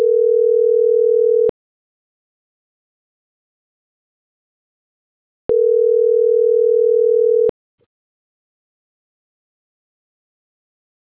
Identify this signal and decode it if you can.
Ringback tone (North America), 440 + 480 Hz.
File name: CENTRAL SAGOT / out isai.hernandez - .86998